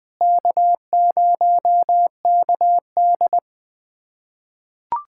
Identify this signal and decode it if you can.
Morse code, heard as a keyed tone, about 20 wpm, 700 Hz. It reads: K0KD
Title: USA Callsign DE Callsign - 4 Characters - 20wpm